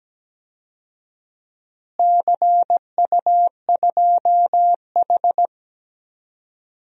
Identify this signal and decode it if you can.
Morse code, heard as a keyed tone, about 17 wpm, 700 Hz. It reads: CU2H
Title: DXCC-100 Call Signs - 4 Characters - Repeated 3x - 17wpm